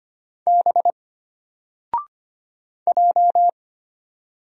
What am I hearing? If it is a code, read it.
Morse code, heard as a keyed tone, about 25 wpm, 700 Hz. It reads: B J
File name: Single-Letter-Number-25wpm